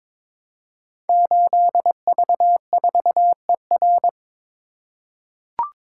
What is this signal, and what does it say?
Morse code, heard as a keyed tone, about 22 wpm, 700 Hz. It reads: 8V4ER
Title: Code Groups - 5 Letter:Numbers - 22wpm